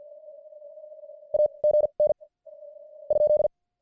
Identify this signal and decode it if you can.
Morse code, heard as a keyed tone, about 38 wpm, spaced as for 30 wpm, 600 Hz. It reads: AGN ?